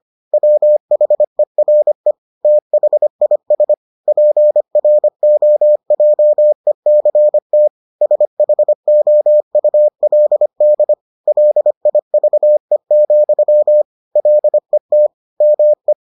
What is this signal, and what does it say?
Morse code, heard as a keyed tone, about 25 wpm, 600 Hz. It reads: WHERE THIS PROJECT SHOULD LIVE, LET ME